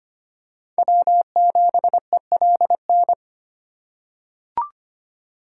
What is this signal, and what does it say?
Morse code, heard as a keyed tone, about 25 wpm, 700 Hz. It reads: W7ELN
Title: USA Call Signs - 5 Characters - 2x Character Spacing - 25wpm